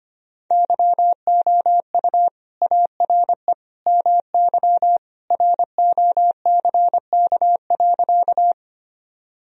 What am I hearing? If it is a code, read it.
Morse code, heard as a keyed tone, about 25 wpm, 700 Hz. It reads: YOU ARE MY ROCK.
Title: Sentences from Top 1000 Words - Review Part 2 - 25wpm